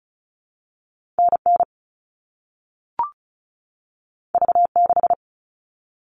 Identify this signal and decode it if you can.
Morse code, heard as a keyed tone, about 36 wpm, 700 Hz. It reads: NN V6
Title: All Cut Numbers - 2-Digit 35wpm